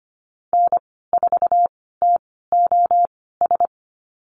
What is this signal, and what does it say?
Morse code, heard as a keyed tone, about 25 wpm, spaced as for 16 wpm, 700 Hz. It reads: N4TOS
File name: Lesson 15g - Number 5 - Call Signs 25wpm